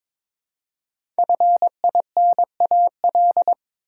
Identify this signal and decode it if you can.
Morse code, heard as a keyed tone, about 22 wpm, 700 Hz. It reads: FINAL
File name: 5 Letter Words - First 3 Letters Given - 22wpm